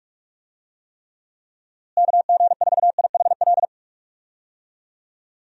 Morse code, heard as a keyed tone, about 46 wpm, 700 Hz. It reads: KG4IHL
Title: USA Call Signs - 6 Characters - Repeated 2x - 45wpm